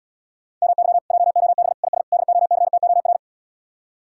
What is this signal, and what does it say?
Morse code, heard as a keyed tone, about 79 wpm, 700 Hz. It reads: MY JOB IS DOCTOR